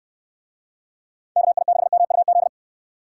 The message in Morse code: KI6MUZ

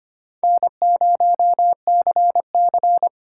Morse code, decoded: N0CC

